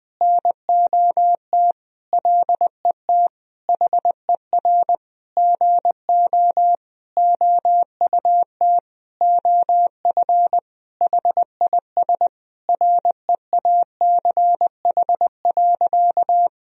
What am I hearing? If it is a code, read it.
Morse code, heard as a keyed tone, about 20 wpm, 700 Hz. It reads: NOT LET HER GO OUT OF HIS REACH.
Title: Sentences from Top 1000 Words - Review Part 2 - 20wpm